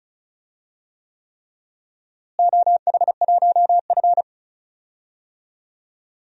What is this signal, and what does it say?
Morse code, heard as a keyed tone, about 36 wpm, 700 Hz. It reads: OH1F
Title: DXCC-100 Call Signs - 4 Characters - Repeated 3x - 35wpm